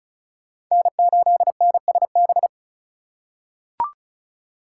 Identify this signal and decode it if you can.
Morse code, heard as a keyed tone, about 36 wpm, 700 Hz. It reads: N8NSB
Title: USA Call Signs - 5 Characters - 2x Character Spacing - 35wpm